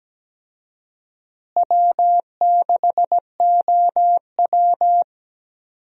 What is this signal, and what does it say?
Morse code, heard as a keyed tone, about 17 wpm, 700 Hz. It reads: W6OW
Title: USA Call Signs - 4 Characters - Repeated 3x - 17wpm